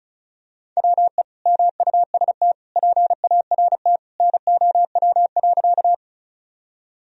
Morse code, heard as a keyed tone, about 36 wpm, 700 Hz. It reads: WE MUST PART NOW.
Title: Sentences from Top 1000 Words - Review Part 1 - 35wpm